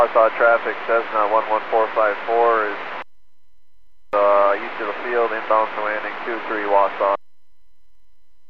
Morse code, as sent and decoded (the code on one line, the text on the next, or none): none